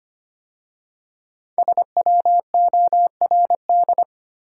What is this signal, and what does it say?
Morse code, heard as a keyed tone, about 25 wpm, 700 Hz. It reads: SWORD
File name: 5 Letter Words - First 3 Letters Given - 25wpm